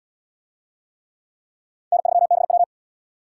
Morse code, heard as a keyed tone, about 66 wpm, 700 Hz. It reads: N3BW